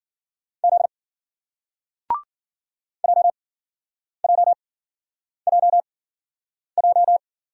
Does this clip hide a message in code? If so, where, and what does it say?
Yes: Morse code from 0.6 to 7.2 s, heard as a keyed tone, about 54 wpm, spaced as for 9 wpm, 700 Hz. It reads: Z JJJJ